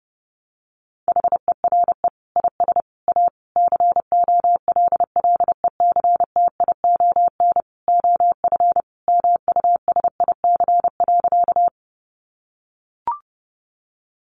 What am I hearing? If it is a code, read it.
Morse code, heard as a keyed tone, about 30 wpm, 700 Hz. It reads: HERE IS A COLLECTION OF MUSIC.
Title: Sentences from Top 1200 Words 30wpm